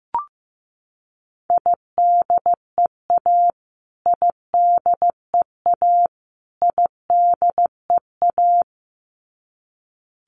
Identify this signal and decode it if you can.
Morse code, heard as a keyed tone, about 15 wpm, 700 Hz. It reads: IDEA IDEA IDEA
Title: Top 500 Words Repeated 15wpm